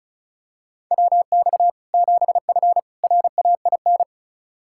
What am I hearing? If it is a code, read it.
Morse code, heard as a keyed tone, about 36 wpm, 700 Hz. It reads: WX 7F RAIN
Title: K1USN SST QSO Practice 35wpm